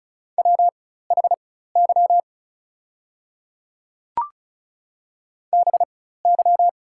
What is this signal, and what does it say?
Morse code, heard as a keyed tone, about 36 wpm, spaced as for 16 wpm, 690 Hz. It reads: WHY BY
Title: 4x Character Spacing - Top 500 Words - 35wpm